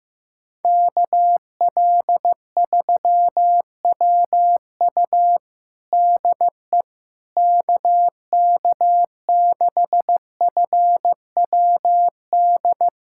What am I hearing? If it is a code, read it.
Morse code, heard as a keyed tone, about 15 wpm, 700 Hz. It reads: KL3WU DE KK6FWD
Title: USA Callsign DE Callsign 15wpm